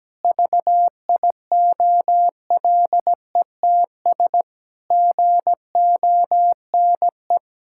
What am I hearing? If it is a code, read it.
Morse code, heard as a keyed tone, about 17 wpm, 700 Hz. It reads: VIOLETS GONE
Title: Pangram Challenge v2 - 17wpm